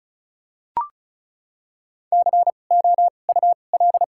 Morse code, heard as a keyed tone, about 36 wpm, spaced as for 25 wpm, 700 Hz. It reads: COUL
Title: 2x Character Spacing - Top 100 Words - 35wpm